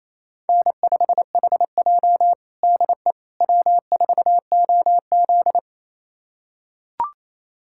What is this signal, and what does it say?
Morse code, heard as a keyed tone, about 28 wpm, 700 Hz. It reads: N5HJ DE W4OZ
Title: USA Callsign DE Callsign - 4 Characters - 28wpm